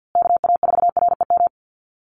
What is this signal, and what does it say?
Morse code, heard as a keyed tone, about 50 wpm, 700 Hz. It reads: KA4REM
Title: USA Call Signs Repeated 2x 50wpm